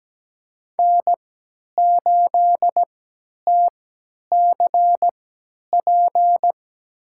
Morse code, heard as a keyed tone, about 17 wpm, spaced as for 10 wpm, 700 Hz. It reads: N8TCP
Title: USA Call Signs - 5 Characters - 3x Character Spacing - 17wpm